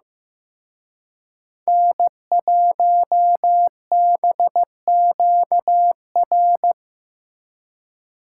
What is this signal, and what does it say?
Morse code, heard as a keyed tone, about 17 wpm, 700 Hz. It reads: N1BQR
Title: Speed Racing - USA Call Signs 15wpm